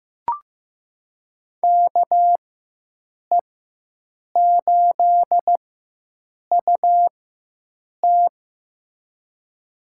Morse code, heard as a keyed tone, about 15 wpm, spaced as for 7 wpm, 700 Hz. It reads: KE8UT
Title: USA Call Signs - 5 Characters - 4x Character Spacing - 15wpm